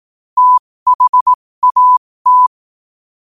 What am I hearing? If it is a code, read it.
Morse code, heard as a keyed tone, about 18 wpm, spaced as for 15 wpm, 1000 Hz. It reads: THAT
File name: qotdfiles / 15wpm